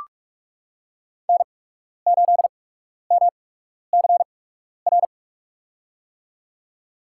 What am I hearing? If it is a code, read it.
Morse code, heard as a keyed tone, about 44 wpm, spaced as for 12 wpm, 690 Hz. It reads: N8MCR
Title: USA Call Signs - 5 Characters - 8x Character Spacing - 45wpm